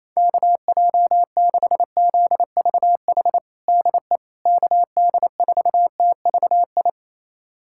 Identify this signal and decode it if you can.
Morse code, heard as a keyed tone, about 28 wpm, 700 Hz. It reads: KJ6ZVH DE KD4TVI
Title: USA Callsign DE Callsign 28wpm